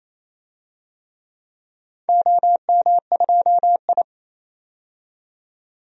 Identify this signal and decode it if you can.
Morse code, heard as a keyed tone, about 28 wpm, 700 Hz. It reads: OM2I
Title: DXCC-100 Call Signs - 4 Characters - 28wpm